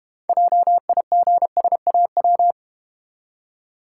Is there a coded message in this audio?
Yes: Morse code from 0.3 to 2.5 s, heard as a keyed tone, about 32 wpm, 700 Hz. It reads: JIGSAW